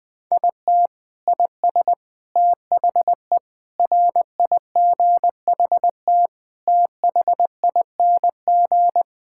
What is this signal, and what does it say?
Morse code, heard as a keyed tone, about 20 wpm, 700 Hz. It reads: IT IS THE RIGHT THING
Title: Sentences with 5-Word Phrases - 20wpm